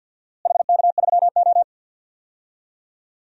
Morse code, heard as a keyed tone, about 52 wpm, 700 Hz. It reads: HK3O
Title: DXCC-100 Call Signs - 4 Characters - Repeated 2x - 50wpm